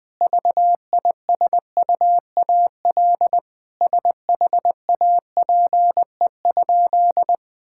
Morse code, heard as a keyed tone, about 20 wpm, 700 Hz. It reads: VISUAL SHAPE?
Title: Sentences from Top 1000 Words - Review Part 6 - 20wpm